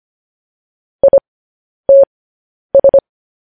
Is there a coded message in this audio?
Yes: Morse code from 1.0 to 3.0 s, heard as a keyed tone, about 25 wpm, spaced as for 10 wpm, 550 Hz. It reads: ITS